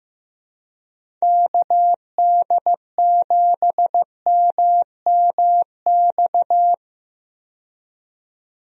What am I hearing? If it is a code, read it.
Morse code, heard as a keyed tone, about 15 wpm, 700 Hz. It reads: KD7MMX
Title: USA Call Signs - 6 Characters - 15wpm